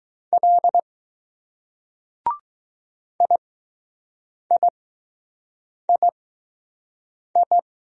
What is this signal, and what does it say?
Morse code, heard as a keyed tone, about 20 wpm, spaced as for 6 wpm, 700 Hz. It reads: L IIII